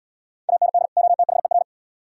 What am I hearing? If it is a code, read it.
Morse code, heard as a keyed tone, about 76 wpm, 700 Hz. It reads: AND OTHER